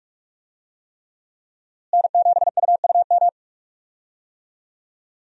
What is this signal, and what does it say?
Morse code, heard as a keyed tone, about 46 wpm, 700 Hz. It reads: N7UUM
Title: USA Call Signs - 5 Characters - Repeated 3x - 45wpm